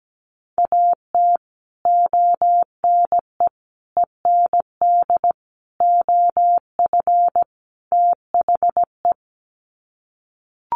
Morse code, heard as a keyed tone, about 17 wpm, 700 Hz. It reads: AT ONE END OF THE
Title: Sets of 5 Words Encore 17wpm